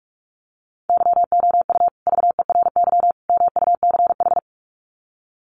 Morse code, heard as a keyed tone, about 44 wpm, 700 Hz. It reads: YOU VERY MUCH
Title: Sets of 3 Words 45wpm